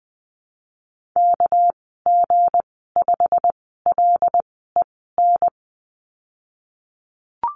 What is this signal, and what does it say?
Morse code, heard as a keyed tone, about 20 wpm, spaced as for 15 wpm, 700 Hz. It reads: KG5LEN
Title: 231 Lesson 37g - Letter x - Call Signs 20wpm